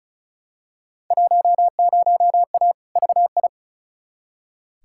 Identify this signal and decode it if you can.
Morse code, heard as a keyed tone, about 36 wpm, 700 Hz. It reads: 10A VI
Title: Field Day Exchanges Encore - 35wpm